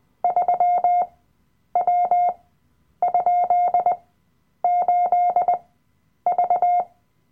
Morse code, heard as a keyed tone, about 20 wpm, spaced as for 9 wpm, 710 Hz. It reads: 3W?8V
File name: Workshop Dateien / CW-Online-Kurs / Lektion 007